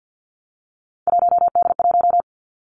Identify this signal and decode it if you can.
Morse code, heard as a keyed tone, about 50 wpm, 700 Hz. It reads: 1D1
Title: Three Challenging Letter-Number Combinations 50wpm